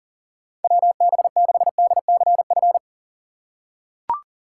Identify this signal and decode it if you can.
Morse code, heard as a keyed tone, about 40 wpm, 700 Hz. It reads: WB6DCF
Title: USA Call Signs - 6 Characters - 3x Character Spacing - 40wpm